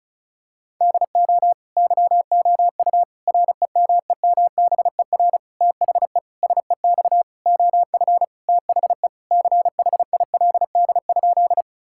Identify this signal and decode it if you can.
Morse code, heard as a keyed tone, about 36 wpm, 700 Hz. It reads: DO YOU REMEMBER THE SEX OF THE CHILD?
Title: Sentences from Top 1600 Words 35wpm